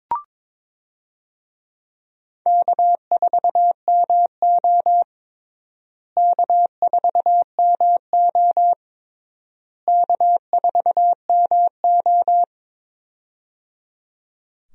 Morse code, heard as a keyed tone, about 22 wpm, 700 Hz. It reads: K4MO K4MO K4MO
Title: USA Call Signs - 4 Characters - Repeated 3x - 22wpm